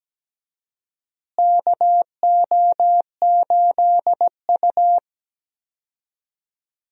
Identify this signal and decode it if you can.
Morse code, heard as a keyed tone, about 17 wpm, 700 Hz. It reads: KO8U
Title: USA Call Signs - 4 Characters - Repeated 3x - 17wpm